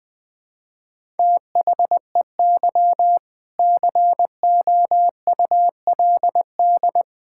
Morse code, heard as a keyed tone, about 20 wpm, 700 Hz. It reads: THEY COULD
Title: Sentences with 7-Word Phrases - 20wpm